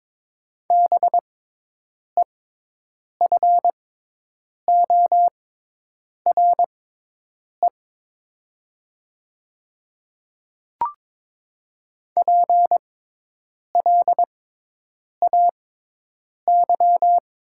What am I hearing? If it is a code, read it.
Morse code, heard as a keyed tone, about 22 wpm, spaced as for 8 wpm, 700 Hz. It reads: BEFORE PLAY
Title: 6x Character Spacing - Top 200 Words - 22wpm